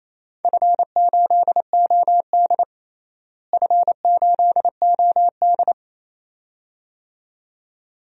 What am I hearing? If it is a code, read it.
Morse code, heard as a keyed tone, about 28 wpm, 700 Hz. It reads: F8OD F8OD
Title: DXCC-100 Call Signs - 4 Characters - Repeated 2x - 28wpm